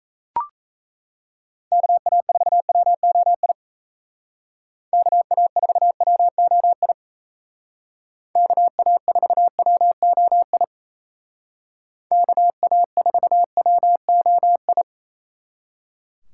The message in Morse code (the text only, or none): KA4WOI KA4WOI KA4WOI KA4WOI